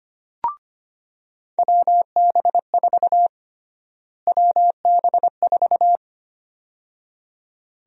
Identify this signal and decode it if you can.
Morse code, heard as a keyed tone, about 25 wpm, 700 Hz. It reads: WB4 WB4
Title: USA Call Sign Prefixes - Repeated 2x - 25wpm